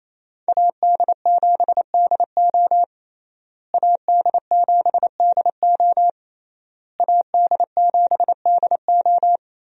Morse code, heard as a keyed tone, about 28 wpm, 700 Hz. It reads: AD7DO AD7DO AD7DO
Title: USA Call Signs - 5 Characters - Repeated 3x - 28wpm